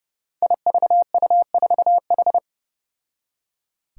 Morse code, heard as a keyed tone, about 30 wpm, 700 Hz. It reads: IVU4H
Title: Five Challenging Letter-Number Combinations 30wpm